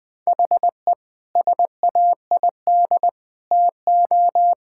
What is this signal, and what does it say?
Morse code, heard as a keyed tone, about 20 wpm, 700 Hz. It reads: HE SAID TO
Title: Sentences from Top 1600 Words 20wpm